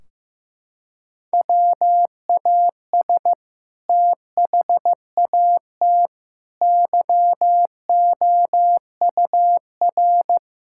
Morse code, heard as a keyed tone, about 15 wpm, 700 Hz. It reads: WAS THAT YOUR
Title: Sentences from Top 1600 Words 15wpm